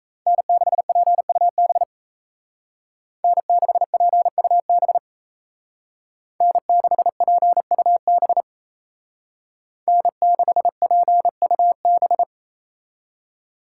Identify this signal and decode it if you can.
Morse code, heard as a keyed tone, about 35 wpm, 700 Hz. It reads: N6PUB N6PUB N6PUB N6PUB
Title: Speed Racing - USA 1x3 Call Signs 28wpm